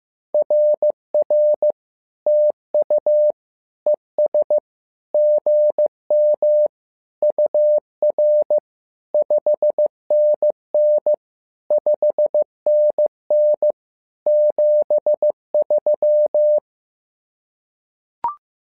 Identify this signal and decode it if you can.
Morse code, heard as a keyed tone, about 15 wpm, 600 Hz. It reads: RR TU ES GM UR 5NN 5NN 73
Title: SOTA QSO Practice 15wpm